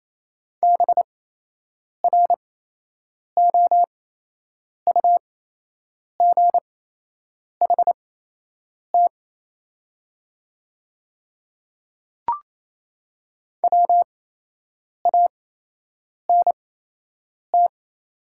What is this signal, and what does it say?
Morse code, heard as a keyed tone, about 27 wpm, spaced as for 8 wpm, 700 Hz. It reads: BROUGHT WANT
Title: 8x Character Spacing - Top 500 Words - 28wpm